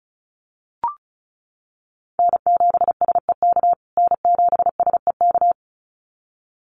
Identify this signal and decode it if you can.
Morse code, heard as a keyed tone, about 34 wpm, 700 Hz. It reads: N7SEK N7SEK
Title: USA Call Signs Repeated 2x 35wpm